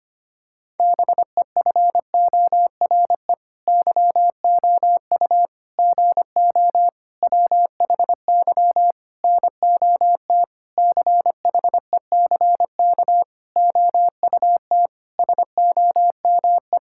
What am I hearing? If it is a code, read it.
Morse code, heard as a keyed tone, about 25 wpm, 700 Hz. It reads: BEFORE YOU GO WHY NOT CHECK OUT SOME
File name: practice / 25wpm / Sentences with 9-Word Phrases - 25wpm